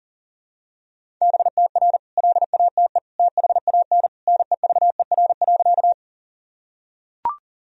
Morse code, heard as a keyed tone, about 40 wpm, 700 Hz. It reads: BTR LATE THAN NEVER.
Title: Sentences from Top 400 Words and Abbreviations 40wpm